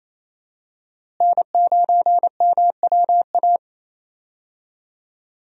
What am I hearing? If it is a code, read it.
Morse code, heard as a keyed tone, about 28 wpm, 700 Hz. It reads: N9MWA
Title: USA Call Signs - 5 Characters - Repeated 3x - 28wpm